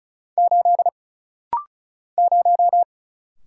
Morse code, heard as a keyed tone, about 36 wpm, 700 Hz. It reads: 8 0
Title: Single Letter-Number - Rapid-Fire 35wpm